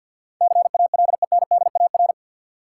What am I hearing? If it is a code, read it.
Morse code, heard as a keyed tone, about 50 wpm, 700 Hz. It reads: CALENDAR
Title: Sentences from Top 1600 Words - New Word Drill Alt - 50wpm